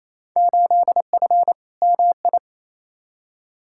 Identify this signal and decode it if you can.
Morse code, heard as a keyed tone, about 28 wpm, 700 Hz. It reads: 8F MI
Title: Field Day Exchanges Encore - 28wpm